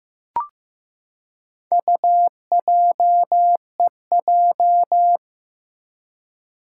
Morse code, heard as a keyed tone, about 15 wpm, 700 Hz. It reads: UJEJ
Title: Four Challenging Letter-Number Combinations 15wpm